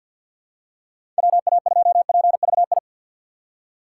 Morse code, heard as a keyed tone, about 47 wpm, 700 Hz. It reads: WA2PVI
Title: USA Call Signs - 6 Characters - 4x Character Spacing - 50wpm